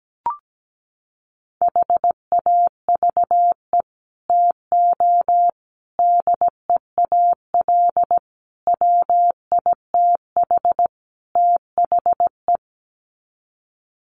Morse code, heard as a keyed tone, about 17 wpm, 700 Hz. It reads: HAVE TO DEAL WITH THE
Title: Sets of 5 Words Encore2 17wpm